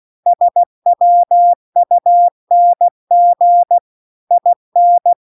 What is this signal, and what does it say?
Morse code, heard as a keyed tone, about 16 wpm, 700 Hz. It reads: SWUNG IN
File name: CW-The-Big-Twoheaded-River-2-16WPM